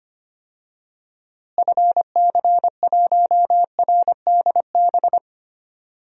Morse code, heard as a keyed tone, about 25 wpm, 700 Hz. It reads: FC1RDB
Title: DXCC-100 Call Signs - 6 Characters - Repeated 3x - 25wpm